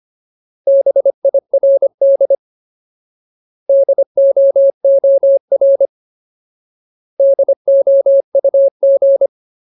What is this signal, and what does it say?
Morse code, heard as a keyed tone, about 25 wpm, 550 Hz. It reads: BIRD DOOR DOUG
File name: Ultimate Word List / 25WPM Ultimate